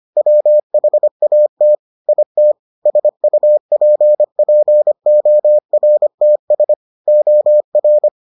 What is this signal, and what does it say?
Morse code, heard as a keyed tone, about 25 wpm, 600 Hz. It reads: WHAT IT SUPPORTS OR